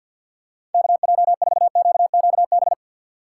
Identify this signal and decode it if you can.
Morse code, heard as a keyed tone, about 53 wpm, 700 Hz. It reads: KJ4QQB
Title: USA Call Signs - 6 Characters - 3x Character Spacing - 50wpm